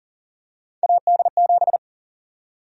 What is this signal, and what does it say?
Morse code, heard as a keyed tone, about 40 wpm, 700 Hz. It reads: AD7
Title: USA Call Sign Prefixes - 40wpm